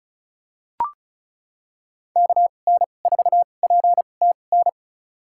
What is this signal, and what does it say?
Morse code, heard as a keyed tone, about 36 wpm, spaced as for 25 wpm, 700 Hz. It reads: KN4PTN
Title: USA Call Signs - 6 Characters - 2x Character Spacing - 35wpm